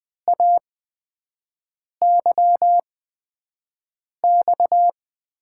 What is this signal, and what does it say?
Morse code, heard as a keyed tone, about 20 wpm, spaced as for 5 wpm, 700 Hz. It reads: AYX
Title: USA Call Signs - 6 Characters - 8x Character Spacing - 20wpm